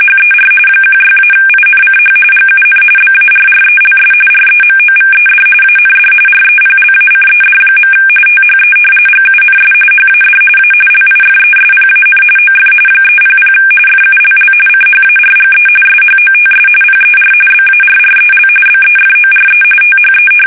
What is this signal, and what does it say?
RTTY, about 45.45 baud. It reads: HER 16-INCH GUN BARRELS REMOVED AND REPLACED. THE GUNS HAD A LIFESPAN OF 300 FULL CHARGE SHOTS PER BARREL BEFORE THE
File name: fsk850 / battleshipnj